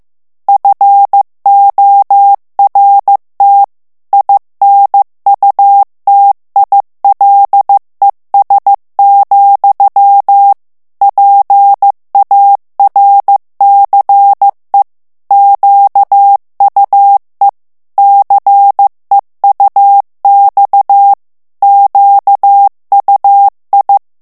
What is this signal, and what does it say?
Morse code, heard as a keyed tone, about 15 wpm, 800 Hz. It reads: FORT INUTILES, PARCE QUE CEUX QUI
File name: FRANCAIS TEXTE CLAIR 2 VITESSE 14 MOTS MINUTES